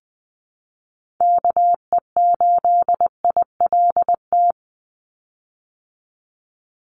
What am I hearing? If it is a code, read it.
Morse code, heard as a keyed tone, about 19 wpm, 700 Hz. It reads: KE8ILT
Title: USA Call Signs 20wpm